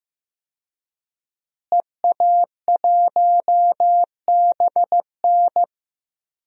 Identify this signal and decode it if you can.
Morse code, heard as a keyed tone, about 15 wpm, 700 Hz. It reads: EA1BN
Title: DXCC-100 Call Signs - 5 Characters - Repeated 2x - 15wpm